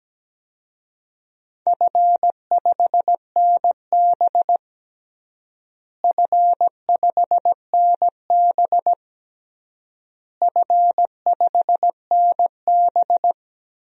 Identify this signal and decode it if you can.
Morse code, heard as a keyed tone, about 17 wpm, 700 Hz. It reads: F5NB F5NB F5NB
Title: DXCC-100 Call Signs - 4 Characters - Repeated 3x - 17wpm